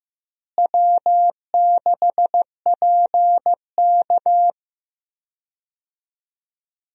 Morse code, heard as a keyed tone, about 15 wpm, 700 Hz. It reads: W6PK
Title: USA Call Signs - 4 Characters - Repeated 2x - 15wpm